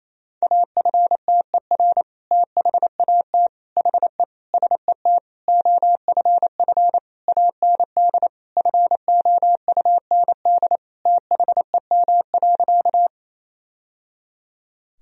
Morse code, heard as a keyed tone, about 28 wpm, 700 Hz. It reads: AFTER THAT HE SET OFF AND FOUND THEM.
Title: Sentences from Top 1000 Words - Review Part 1 - 28wpm